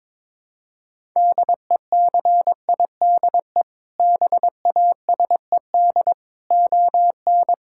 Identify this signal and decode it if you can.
Morse code, heard as a keyed tone, about 22 wpm, 700 Hz. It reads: DECIDE BASED ON
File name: practice / 22wpm / Sentences with 2-Word Phrases - 22wpm